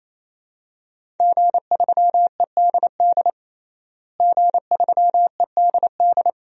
Morse code, heard as a keyed tone, about 28 wpm, 700 Hz. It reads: G3EDD G3EDD
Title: DXCC-100 Call Signs - 5 Characters - Repeated 3x - 28wpm